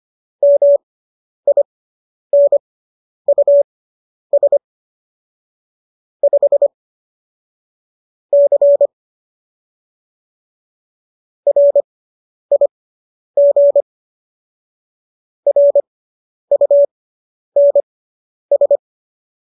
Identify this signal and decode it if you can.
Morse code, heard as a keyed tone, about 25 wpm, spaced as for 10 wpm, 570 Hz. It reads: MINUS 5 C RIG RUNS